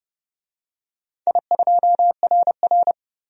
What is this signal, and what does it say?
Morse code, heard as a keyed tone, about 30 wpm, 700 Hz. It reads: I2RR
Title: DXCC-100 Call Signs - 4 Characters - 30wpm